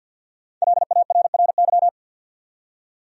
Morse code, heard as a keyed tone, about 49 wpm, 700 Hz. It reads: LARRY
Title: USA Common Names - Max 5 Letters - 50wpm